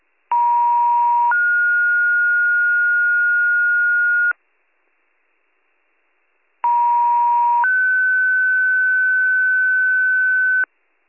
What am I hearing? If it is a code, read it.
Two-tone page, heard as a one-second tone, then a three-second tone, about 950 then 1430 Hz, about 950 then 1500 Hz, with noise. It is heard from 0.3 to 10.6 s.